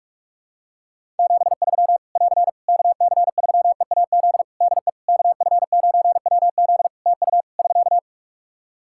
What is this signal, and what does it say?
Morse code, heard as a keyed tone, about 42 wpm, 700 Hz. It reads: 73 <AR> KC3EAZ DE KF9WZ TU <SK>
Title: Practice QSO - Exchange 4 - Ending - 45wpm